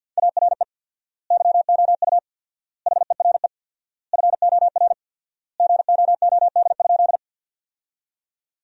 Morse code, heard as a keyed tone, about 48 wpm, 700 Hz. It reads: ARE YOU HERE FOR GOOD?
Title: 4x Word-Spacing - Sentences from Top 200 Words - 50wpm